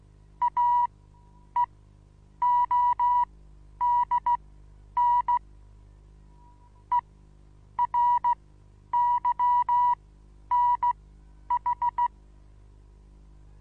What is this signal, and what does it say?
Morse code, heard as a keyed tone, about 16 wpm, spaced as for 9 wpm, 970 Hz. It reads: AEODN ERYNH
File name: Onlinelehrgang / morsen / DJ4UF Morsen25